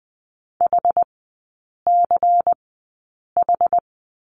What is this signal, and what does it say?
Morse code, heard as a keyed tone, about 20 wpm, spaced as for 8 wpm, 700 Hz. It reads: HCH
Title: 093 Lesson 16e - Letter c - Three Characters 20wpm